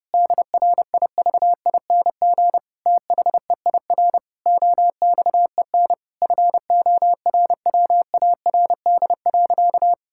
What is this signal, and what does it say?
Morse code, heard as a keyed tone, about 30 wpm, 700 Hz. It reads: DRIVING THEIR OXEN FORWARD.
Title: Pangram Challenge v2 - 30wpm